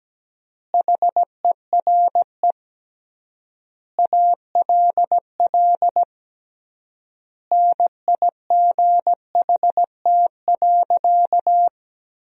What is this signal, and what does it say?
Morse code, heard as a keyed tone, about 17 wpm, 700 Hz. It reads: HERE ALL NIGHT.
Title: 3x Word-Spacing - Sentences from Top 300 Words - 17wpm